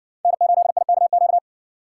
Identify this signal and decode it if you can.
Morse code, heard as a keyed tone, about 59 wpm, 700 Hz. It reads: N8IKQ